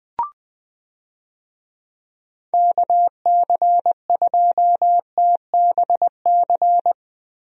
Morse code, heard as a keyed tone, about 20 wpm, 700 Hz. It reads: KC2TBC
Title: USA Call Signs - 6 Characters - Repeated 3x - 20wpm